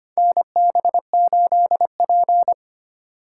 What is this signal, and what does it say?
Morse code, heard as a keyed tone, about 25 wpm, 700 Hz. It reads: NB8P
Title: USA Call Signs - 4 Characters - 2x Character Spacing - 25wpm